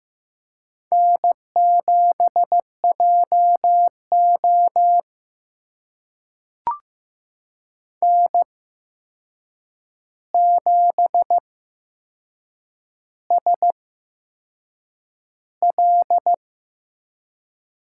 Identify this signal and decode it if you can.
Morse code, heard as a keyed tone, about 15 wpm, spaced as for 6 wpm, 700 Hz. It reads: N7JO N7SL
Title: USA Call Signs - 8x Character Spacing - 15wpm